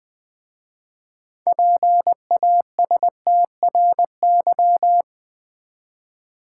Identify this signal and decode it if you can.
Morse code, heard as a keyed tone, about 20 wpm, 690 Hz. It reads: PASTRY
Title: Easy Words with Hints - 20wpm